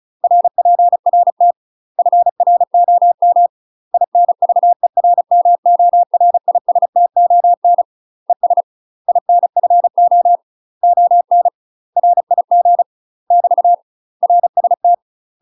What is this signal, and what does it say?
Morse code, heard as a keyed tone, about 34 wpm, 700 Hz. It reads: RPRT FROM INVERMORISTON ES INFO ON RIG <BT> RST
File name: UK MASTER QSOs TRAINING / 35WPM / UK MASTER QSOs 0011